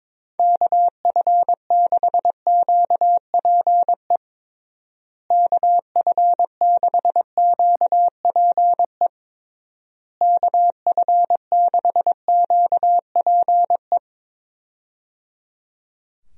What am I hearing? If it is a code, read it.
Morse code, heard as a keyed tone, about 22 wpm, 700 Hz. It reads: KF6QPE KF6QPE KF6QPE
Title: USA Call Signs - 6 Characters - Repeated 3x - 22wpm